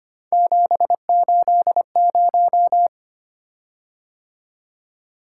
Morse code, heard as a keyed tone, about 25 wpm, 700 Hz. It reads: 780
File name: Three Letter-Number Combinations - 25wpm